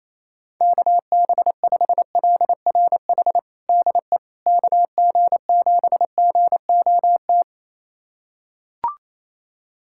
Morse code, heard as a keyed tone, about 28 wpm, 700 Hz. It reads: KB5LRH DE KG7GOT
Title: USA Callsign DE Callsign 28wpm